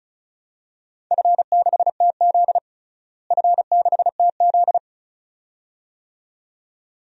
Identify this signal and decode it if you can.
Morse code, heard as a keyed tone, about 36 wpm, 700 Hz. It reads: F6TZ F6TZ
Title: DXCC-100 Call Signs - 4 Characters - Repeated 2x - 35wpm